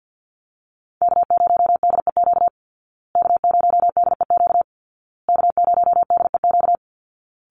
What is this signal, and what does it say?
Morse code, heard as a keyed tone, about 50 wpm, 700 Hz. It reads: K0DEQ K0DEQ K0DEQ
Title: Call Signs Repeated 2x - 50wpm